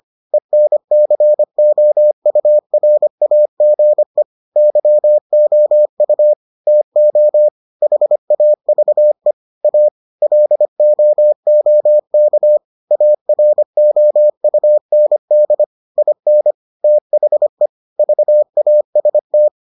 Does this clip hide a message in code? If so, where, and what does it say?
Yes: Morse code from 0.3 to 19.5 s, heard as a keyed tone, about 25 wpm, 600 Hz. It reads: ENCOURAGE YOU TO HAVE A LOOK AROUND IN THE VAST